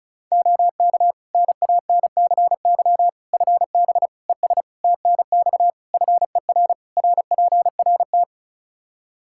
Morse code, heard as a keyed tone, about 36 wpm, 700 Hz. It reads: OK NANCY FB ES TNX FER RPRT
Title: Practice QSO - Exchange 2 - Info - 35wpm